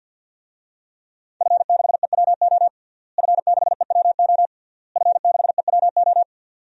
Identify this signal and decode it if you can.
Morse code, heard as a keyed tone, about 52 wpm, 700 Hz. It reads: F6EWO F6EWO F6EWO
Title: DXCC-100 Call Signs - 5 Characters - Repeated 3x - 50wpm